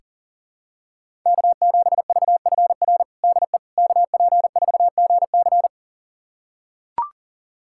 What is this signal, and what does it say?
Morse code, heard as a keyed tone, about 40 wpm, 700 Hz. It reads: K7VFR DE KP4GC
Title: USA Callsign DE Callsign - 5 Characters - 40wpm